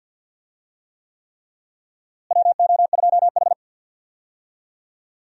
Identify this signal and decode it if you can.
Morse code, heard as a keyed tone, about 51 wpm, 700 Hz. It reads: WO2H
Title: USA Call Signs - 4 Characters - Repeated 3x - 50wpm